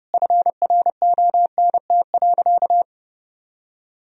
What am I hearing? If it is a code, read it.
Morse code, heard as a keyed tone, about 30 wpm, 700 Hz. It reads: FRONT.
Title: Sentences from Top 1000 Words - Review Part 2 - 30wpm